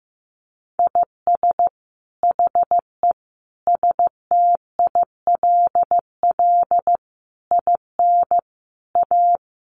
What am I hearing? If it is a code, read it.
Morse code, heard as a keyed tone, about 15 wpm, 700 Hz. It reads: IS HE STILL IN A
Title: Sentences from Top 800 Words 15wpm